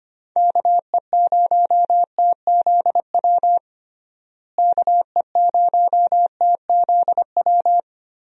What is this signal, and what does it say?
Morse code, heard as a keyed tone, about 25 wpm, 700 Hz. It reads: KE0TZW KE0TZW
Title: USA Call Signs - 6 Characters - Repeated 3x - 25wpm